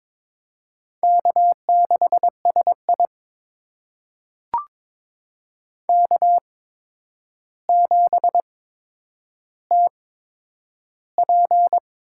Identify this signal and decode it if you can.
Morse code, heard as a keyed tone, about 22 wpm, spaced as for 9 wpm, 700 Hz. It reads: K6SI K7TP
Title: USA Call Signs - 4 Characters - 8x Character Spacing - 22wpm